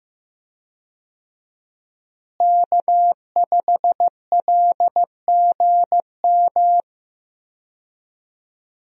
Morse code, heard as a keyed tone, about 15 wpm, 700 Hz. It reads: K5LGM
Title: USA Call Signs - 5 Characters - Repeated 2x - 15wpm